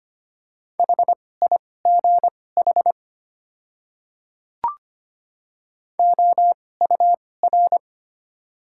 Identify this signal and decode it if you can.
Morse code, heard as a keyed tone, about 25 wpm, spaced as for 18 wpm, 700 Hz. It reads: HIGH OUR
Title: 2x Character Spacing - Top 500 Words - 25wpm